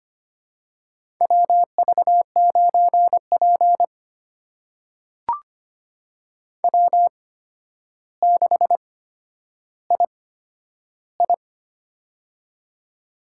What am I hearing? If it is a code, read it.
Morse code, heard as a keyed tone, about 25 wpm, spaced as for 11 wpm, 700 Hz. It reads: WV9P W6II
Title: USA Call Signs - 4 Characters - 8x Character Spacing - 25wpm